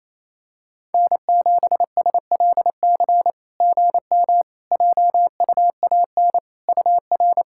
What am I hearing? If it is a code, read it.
Morse code, heard as a keyed tone, about 28 wpm, 690 Hz. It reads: N7SLC GM JUAN UR
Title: SOTA QSO Practice 28wpm